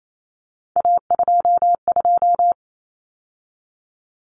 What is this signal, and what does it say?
Morse code, heard as a keyed tone, about 28 wpm, 700 Hz. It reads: A22
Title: Common Cut Numbers - 3-Digit 28wpm